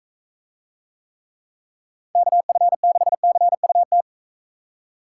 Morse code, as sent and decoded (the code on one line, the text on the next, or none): -.- ..-. -.... -.-. ..- -
KF6CUT